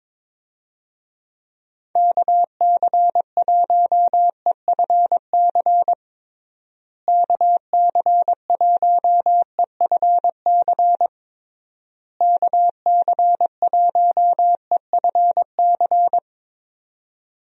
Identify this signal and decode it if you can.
Morse code, heard as a keyed tone, about 22 wpm, 700 Hz. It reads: KC1EFC KC1EFC KC1EFC
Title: USA Call Signs - 6 Characters - Repeated 3x - 22wpm